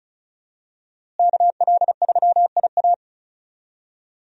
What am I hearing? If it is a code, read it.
Morse code, heard as a keyed tone, about 36 wpm, 700 Hz. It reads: KL3IA
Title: USA Call Signs - 5 Characters - Repeated 3x - 35wpm